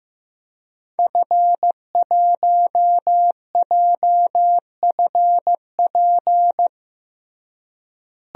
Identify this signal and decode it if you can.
Morse code, heard as a keyed tone, about 15 wpm, 700 Hz. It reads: F1JFP
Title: DXCC-100 Call Signs - 5 Characters - Repeated 2x - 15wpm